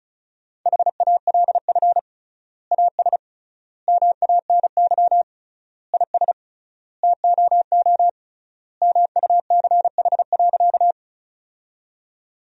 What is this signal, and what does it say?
Morse code, heard as a keyed tone, about 34 wpm, 700 Hz. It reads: HALF AS MANY IS TOO MUCH.
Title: 3x Word-Spacing - Sentences from Top 400 Words - 35wpm